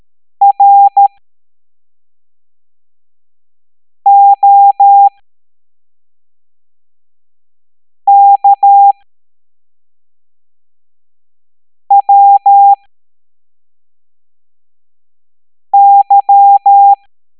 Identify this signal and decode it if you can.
Morse code, heard as a keyed tone, about 13 wpm, spaced as for 3 wpm, 800 Hz. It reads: ROKWY